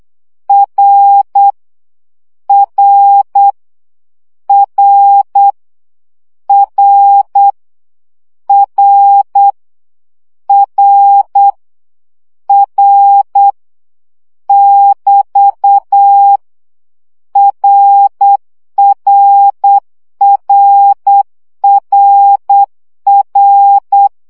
Morse code, heard as a keyed tone, about 8 wpm, 800 Hz. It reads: R R R R R R R <BT> RRRRR
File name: CW gyakorlas r 7 baud